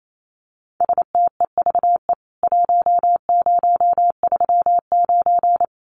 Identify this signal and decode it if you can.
Morse code, heard as a keyed tone, about 28 wpm, 700 Hz. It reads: STEVE 1039
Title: CWT Exchange 28wpm